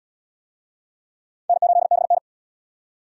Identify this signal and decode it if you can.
Morse code, heard as a keyed tone, about 66 wpm, 700 Hz. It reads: N8BN